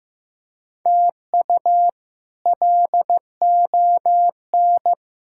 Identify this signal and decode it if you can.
Morse code, heard as a keyed tone, about 15 wpm, 700 Hz. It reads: TU LON
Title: K1USN SST QSO Practice - 15wpm